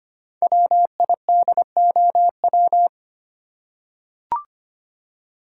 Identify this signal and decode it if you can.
Morse code, heard as a keyed tone, about 25 wpm, 700 Hz. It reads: WIDOW
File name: 5 Letter Words - First 3 Letters Given - 25wpm